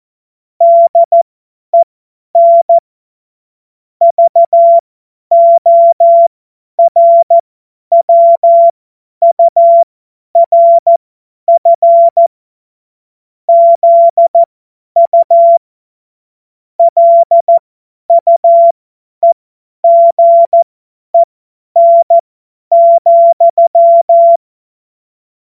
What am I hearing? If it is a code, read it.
Morse code, heard as a keyed tone, about 14 wpm, spaced as for 10 wpm, 680 Hz. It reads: DEN VORWURF ZU LUEGEN,